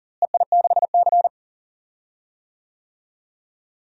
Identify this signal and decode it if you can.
Morse code, heard as a keyed tone, about 39 wpm, 700 Hz. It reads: EI6C
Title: DXCC-100 Call Signs - 4 Characters - Repeated 2x - 40wpm